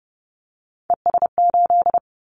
Morse code, heard as a keyed tone, about 30 wpm, 700 Hz. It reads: ES8
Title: Callsign Prefixes for Top 50 DXCC - 30wpm